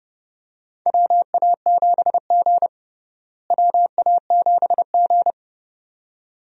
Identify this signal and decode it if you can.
Morse code, heard as a keyed tone, about 32 wpm, 700 Hz. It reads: WA7G WA7G
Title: USA Call Signs - 4 Characters - Repeated 2x - 30wpm